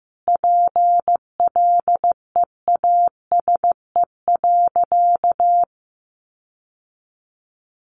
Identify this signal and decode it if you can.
Morse code, heard as a keyed tone, about 15 wpm, 700 Hz. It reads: PLEASE.
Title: Sentences from Top 1500 Words 15wpm